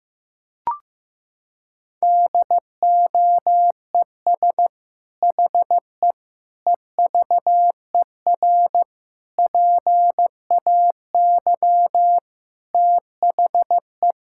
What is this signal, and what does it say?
Morse code, heard as a keyed tone, about 15 wpm, 700 Hz. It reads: DOES HE EVER PAY THE
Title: Sentences from Top 1000 Words - Review Part 5 - 15wpm